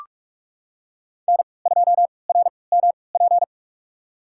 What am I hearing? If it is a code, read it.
Morse code, heard as a keyed tone, about 43 wpm, spaced as for 25 wpm, 700 Hz. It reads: N2RMP
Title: USA Call Signs - 5 Characters - 3x Character Spacing - 45wpm